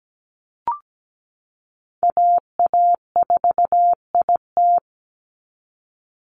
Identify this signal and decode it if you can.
Morse code, heard as a keyed tone, about 17 wpm, 700 Hz. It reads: AA4IT
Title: NA SOTA Chaser Callsigns 17wpm